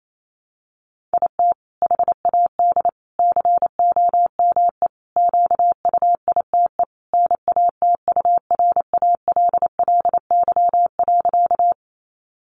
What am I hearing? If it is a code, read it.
Morse code, heard as a keyed tone, about 28 wpm, 700 Hz. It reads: IT HAD COME QUITE NATURALLY.
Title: Sentences from Top 1300 Words 28wpm